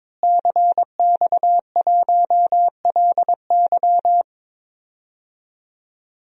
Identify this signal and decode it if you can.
Morse code, heard as a keyed tone, about 22 wpm, 700 Hz. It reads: CX1LY
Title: DXCC-100 Call Signs - 5 Characters - Repeated 2x - 22wpm